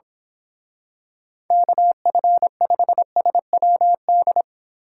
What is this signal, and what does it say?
Morse code, heard as a keyed tone, about 26 wpm, 700 Hz. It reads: KF5SWD
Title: Speed Racing - USA 2x3 Call Signs 22wpm